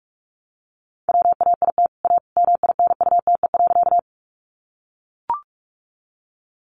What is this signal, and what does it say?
Morse code, heard as a keyed tone, about 44 wpm, 700 Hz. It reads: WAIT A MINUTE.
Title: Sentences from Top 500 Words 45wpm